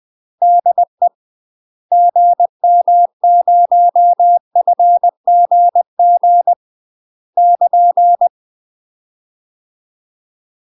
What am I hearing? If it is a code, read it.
Morse code, heard as a keyed tone, about 20 wpm, 700 Hz. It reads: DE GM0FGG <KN>
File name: UK MASTER QSOs TRAINING / 20WPM / UK MASTER QSOs 0072